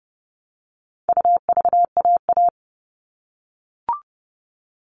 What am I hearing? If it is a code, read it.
Morse code, heard as a keyed tone, about 30 wpm, 700 Hz. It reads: UVAA
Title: All Cut Numbers - 4-Digit 30wpm